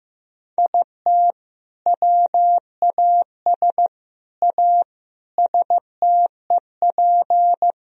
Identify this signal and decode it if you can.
Morse code, heard as a keyed tone, about 14 wpm, 700 Hz. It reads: IT WAS A STEP
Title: Sentences from Top 1000 Words - Review Part 3 - 15wpm